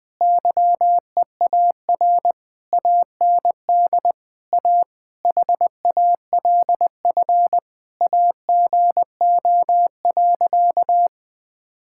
Morse code, heard as a keyed tone, about 20 wpm, 700 Hz. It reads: YEAR AND A HALF AGO.
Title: Sentences with 7-Word Phrases - Spoken Beginning - 20wpm